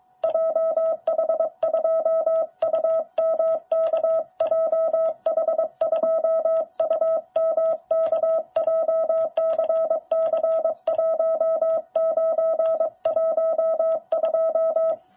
Morse code, heard as a keyed tone, about 23 wpm, 650 Hz. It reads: JH2UMK JH2UMK JCC1912
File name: jh8cbh / 2021cw